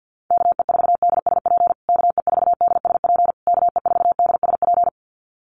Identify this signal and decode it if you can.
Morse code, heard as a keyed tone, about 50 wpm, 700 Hz. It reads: KE4DSP KE4DSP KE4DSP
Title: USA Call Signs Repeated 3x 50wpm